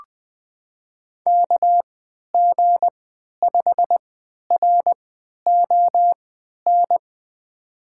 Morse code, heard as a keyed tone, about 20 wpm, spaced as for 11 wpm, 700 Hz. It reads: KG5RON
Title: USA Call Signs - 6 Characters - 3x Character Spacing - 20wpm